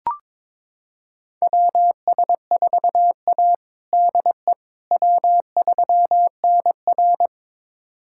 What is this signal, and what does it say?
Morse code, heard as a keyed tone, about 22 wpm, 700 Hz. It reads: WS4A DE W3NR
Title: USA Callsign DE Callsign - 4 Characters - 22wpm